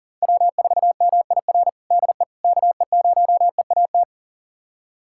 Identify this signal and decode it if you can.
Morse code, heard as a keyed tone, about 40 wpm, 700 Hz. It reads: W4MIR DE KE0EAT
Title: USA Callsign DE Callsign 40wpm